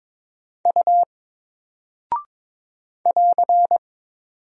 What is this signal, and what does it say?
Morse code, heard as a keyed tone, about 22 wpm, 700 Hz. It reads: U <AR>
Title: Single Character (Letter:Number:Punctuation:Prosign) 22wpm